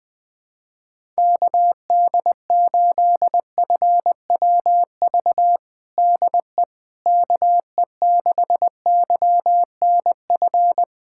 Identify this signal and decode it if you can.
Morse code, heard as a keyed tone, about 19 wpm, 700 Hz. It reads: KD8FWV DE KE6YNF
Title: USA Callsign DE Callsign 20wpm